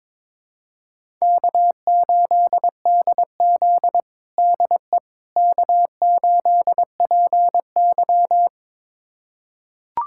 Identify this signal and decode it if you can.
Morse code, heard as a keyed tone, about 21 wpm, 700 Hz. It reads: K8DZ DE K8PY
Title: USA Callsign DE Callsign - 4 Characters - 22wpm